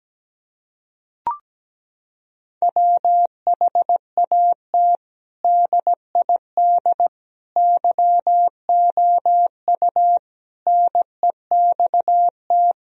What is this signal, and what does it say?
Morse code, heard as a keyed tone, about 17 wpm, 700 Hz. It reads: WHAT DID YOU NEXT
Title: Sentences from Top 1000 Words - Review Part 1 - 17wpm